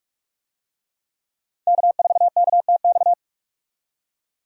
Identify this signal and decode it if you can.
Morse code, heard as a keyed tone, about 42 wpm, 700 Hz. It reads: K4KTX
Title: USA Call Signs - 5 Characters - 45wpm